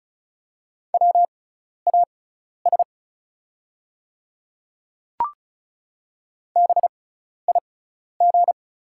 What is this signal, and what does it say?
Morse code, heard as a keyed tone, about 36 wpm, spaced as for 12 wpm, 700 Hz. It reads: WAS BIG
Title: 6x Character Spacing - Top 200 Words - 35wpm